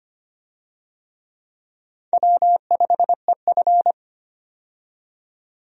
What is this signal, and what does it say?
Morse code, heard as a keyed tone, about 25 wpm, 700 Hz. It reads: W5EF
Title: USA Call Signs - 4 Characters - 25wpm